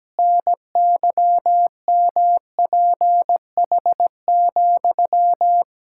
Morse code, heard as a keyed tone, about 17 wpm, 700 Hz. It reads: NYMPH,
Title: Pangram Challenge v2 - 17wpm